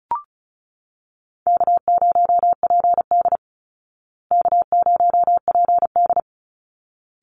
Morse code, heard as a keyed tone, about 34 wpm, 700 Hz. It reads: K0PD K0PD
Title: Easy Call Signs Repeated 2x - 35wpm